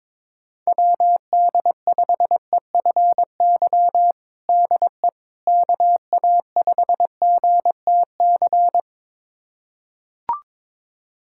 Morse code, heard as a keyed tone, about 22 wpm, 700 Hz. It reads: WD5EFY DE KA5GTC
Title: USA Callsign DE Callsign - 6 Characters - 22wpm